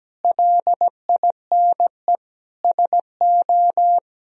Morse code, heard as a keyed tone, about 17 wpm, 700 Hz. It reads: LINE SO
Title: Sentences from Top 1000 Words - Review Part 3 - 17wpm